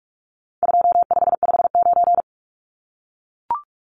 Morse code, heard as a keyed tone, about 44 wpm, 700 Hz. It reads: 2559
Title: Numbers - 4-Digit 45wpm